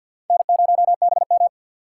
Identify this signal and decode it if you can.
Morse code, heard as a keyed tone, about 52 wpm, 690 Hz. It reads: N0BM
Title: USA Call Signs - 6x Character Spacing - 50wpm